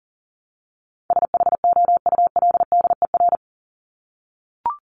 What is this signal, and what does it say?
Morse code, heard as a keyed tone, about 40 wpm, 700 Hz. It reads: SHOULDER
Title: Top 1501-1750 Words 40wpm